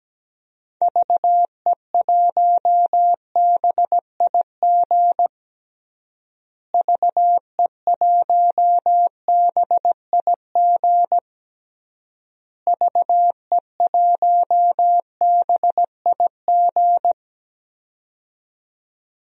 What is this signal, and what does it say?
Morse code, heard as a keyed tone, about 17 wpm, 700 Hz. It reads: VE1BIG VE1BIG VE1BIG
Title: DXCC-100 Call Signs - 6 Characters - Repeated 3x - 17wpm